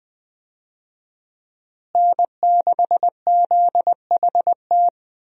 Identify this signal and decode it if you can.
Morse code, heard as a keyed tone, about 21 wpm, 700 Hz. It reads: N6ZHT
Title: USA Call Signs - 5 Characters - Repeated 3x - 20wpm